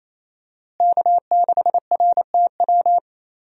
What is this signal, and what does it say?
Morse code, heard as a keyed tone, about 28 wpm, 700 Hz. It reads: K6RTW
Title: USA Call Signs - 5 Characters - Repeated 2x - 28wpm